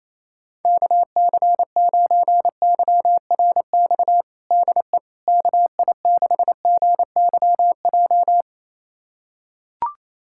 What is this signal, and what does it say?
Morse code, heard as a keyed tone, about 28 wpm, 700 Hz. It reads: KC9YRX DE KI6GYJ
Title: USA Callsign DE Callsign 28wpm